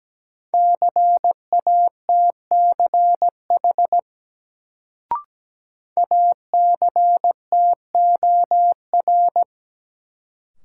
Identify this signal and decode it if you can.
Morse code, heard as a keyed tone, about 17 wpm, 700 Hz. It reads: CATCH ACTOR
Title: Sentences from Top 1500 Words - New Word Drill Alt - 17wpm